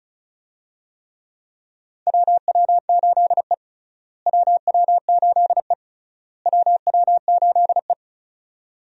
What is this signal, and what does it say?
Morse code, heard as a keyed tone, about 36 wpm, 700 Hz. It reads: WW8E WW8E WW8E
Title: USA Call Signs - 4 Characters - Repeated 3x - 35wpm